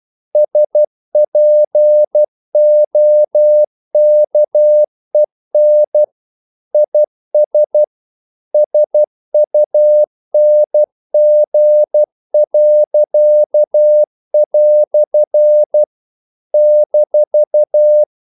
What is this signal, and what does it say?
Morse code, heard as a keyed tone, about 12 wpm, 600 Hz. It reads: SPOKEN IS SUNG." -
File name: CW / Practice / EnglishQuotations0312